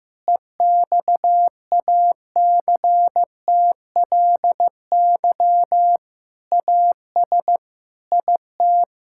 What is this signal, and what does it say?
Morse code, heard as a keyed tone, about 15 wpm, 700 Hz. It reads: EXACTLY AS IT
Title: Sentences from Top 1000 Words - Review Part 6 - 15wpm